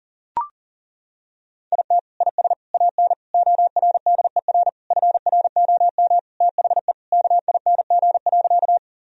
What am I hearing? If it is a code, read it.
Morse code, heard as a keyed tone, about 40 wpm, 700 Hz. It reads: IT IS AN ORDER FROM THE KING.
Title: Sentences from Top 1000 Words - Review Part 2 - 40wpm